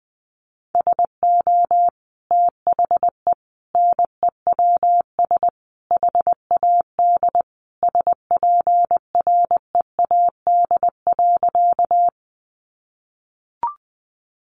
Morse code, heard as a keyed tone, about 20 wpm, 700 Hz. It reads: SO THE NEWS HAD SPREAD.
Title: Sentences from Top 1200 Words 20wpm